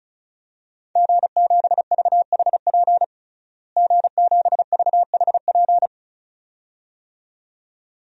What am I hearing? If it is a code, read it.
Morse code, heard as a keyed tone, about 34 wpm, 700 Hz. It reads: G7VHP G7VHP
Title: DXCC-100 Call Signs - 5 Characters - Repeated 2x - 35wpm